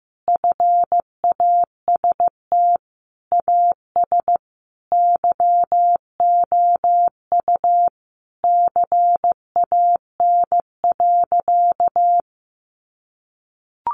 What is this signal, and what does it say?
Morse code, heard as a keyed tone, about 15 wpm, 700 Hz. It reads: FAST AS YOU CAN.
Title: Sentences from Top 600 Words 15wpm